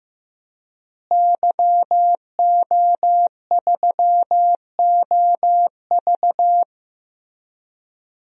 Morse code, heard as a keyed tone, about 15 wpm, 700 Hz. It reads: YO3OV
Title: DXCC-100 Call Signs - 5 Characters - Repeated 3x - 15wpm